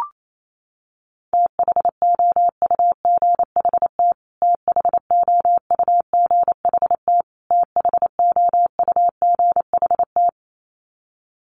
Morse code, heard as a keyed tone, about 28 wpm, 700 Hz. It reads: THOUGHT THOUGHT THOUGHT
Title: Top 500 Words Repeated 28wpm